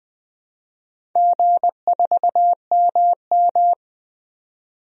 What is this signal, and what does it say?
Morse code, heard as a keyed tone, about 20 wpm, 700 Hz. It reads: G4MM
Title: DXCC-100 Call Signs - 4 Characters - Repeated 2x - 20wpm